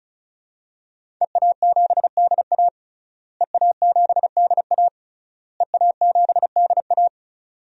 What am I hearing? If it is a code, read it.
Morse code, heard as a keyed tone, about 34 wpm, 700 Hz. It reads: EA7DA EA7DA EA7DA
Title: DXCC-100 Call Signs - 5 Characters - Repeated 3x - 35wpm